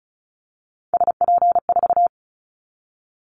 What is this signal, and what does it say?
Morse code, heard as a keyed tone, about 36 wpm, 700 Hz. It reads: SP4
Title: Callsign Prefixes for Top 25 DXCC - 35wpm